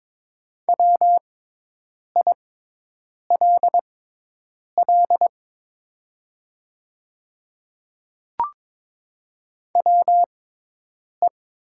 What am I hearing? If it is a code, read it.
Morse code, heard as a keyed tone, about 22 wpm, spaced as for 8 wpm, 700 Hz. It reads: WILL WE
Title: 6x Character Spacing - Top 300 Words - 22wpm